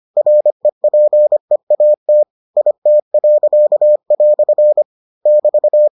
Morse code, heard as a keyed tone, about 25 wpm, 600 Hz. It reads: REPEAT IT." <BT>